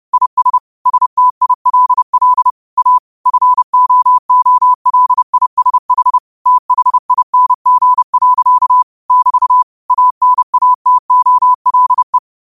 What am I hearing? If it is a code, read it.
Morse code, heard as a keyed tone, about 30 wpm, 1000 Hz. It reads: IS STILL A FOOLISH THING. <BT> ANATOLE
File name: qotdfiles / 30wpm